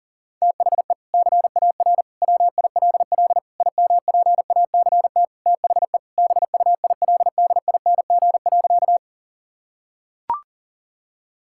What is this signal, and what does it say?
Morse code, heard as a keyed tone, about 37 wpm, 700 Hz. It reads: THE CAR WILL IMPACT THE BUILDING.
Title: Sentences from Top 1000 Words - Review Part 6 - 40wpm